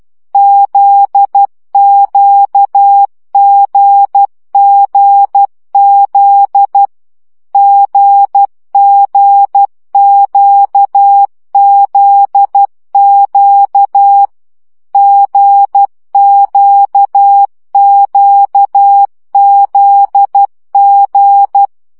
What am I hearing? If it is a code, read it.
Morse code, heard as a keyed tone, about 12 wpm, 800 Hz. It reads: ZQGGZ GGQZQ GQQZG
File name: CW gyakorlas g z q 10 baud